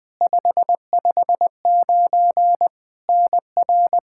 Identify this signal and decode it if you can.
Morse code, heard as a keyed tone, about 20 wpm, 700 Hz. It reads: 559 NR
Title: Practice QSO - Exchange 1 - Beginning - 20wpm